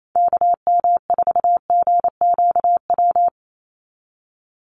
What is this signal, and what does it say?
Morse code, heard as a keyed tone, about 28 wpm, 700 Hz. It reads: KM4GQW
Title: USA Call Signs 28wpm